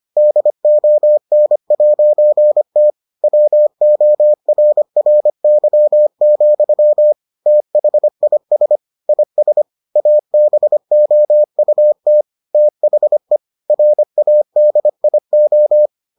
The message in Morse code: DON'T WORRY, THIS IS ABOUT THE RADIO

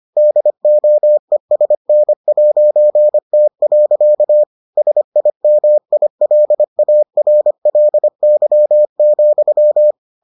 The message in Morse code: DOESN'T. SIMILARLY,

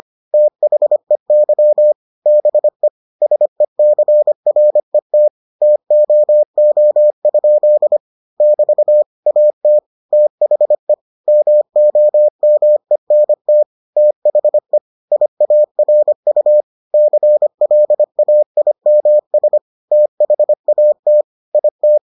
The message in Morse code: THEY BE SECRET TOO? <BT> AT THE MOMENT THE IARU CLAIMS THAT IT

